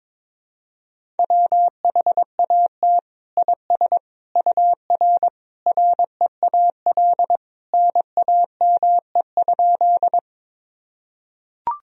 Morse code, heard as a keyed tone, about 22 wpm, 700 Hz. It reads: WHAT IS UR REAL NAME?
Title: Sentences from Top 300 Words and Abbreviations 22wpm